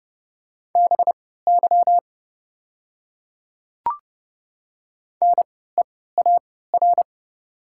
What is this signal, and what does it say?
Morse code, heard as a keyed tone, about 30 wpm, spaced as for 17 wpm, 700 Hz. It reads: BY NEAR
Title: 3x Character Spacing - Top 300 Words - 30wpm